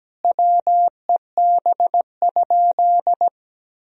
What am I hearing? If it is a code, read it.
Morse code, heard as a keyed tone, about 17 wpm, 700 Hz. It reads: WEB?
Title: Sentences from Top 1000 Words - Review Part 4 - 17wpm